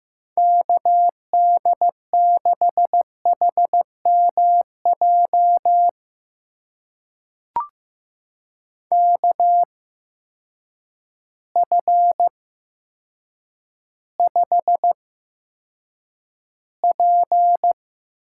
Morse code, heard as a keyed tone, about 15 wpm, spaced as for 8 wpm, 700 Hz. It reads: KD6HMJ KF5P
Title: USA Call Signs - 6 Characters - 8x Character Spacing - 15wpm